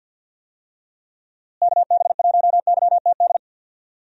Morse code, heard as a keyed tone, about 47 wpm, 700 Hz. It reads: KB1YTD
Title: USA Call Signs - 6 Characters - Repeated 2x - 50wpm